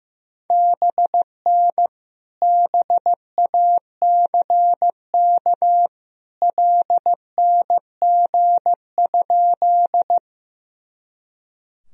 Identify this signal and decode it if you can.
Morse code, heard as a keyed tone, about 16 wpm, 700 Hz. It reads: BN BACK LNG?
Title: Sentences from Top 200 Words and Abbreviations 15wpm